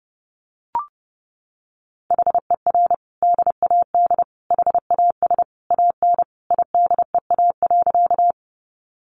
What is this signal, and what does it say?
Morse code, heard as a keyed tone, about 30 wpm, 700 Hz. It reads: HER DAD HAS AN IDEA.
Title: Additional Easy Phrases 30wpm